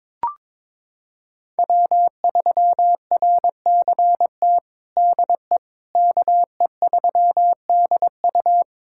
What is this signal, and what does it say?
Morse code, heard as a keyed tone, about 22 wpm, 700 Hz. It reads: W3RCT DE KE3DU
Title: USA Callsign DE Callsign - 5 Characters - 22wpm